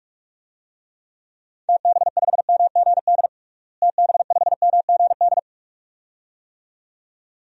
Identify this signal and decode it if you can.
Morse code, heard as a keyed tone, about 46 wpm, 700 Hz. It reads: TB5MGD TB5MGD
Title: DXCC-100 Call Signs - 6 Characters - Repeated 2x - 45wpm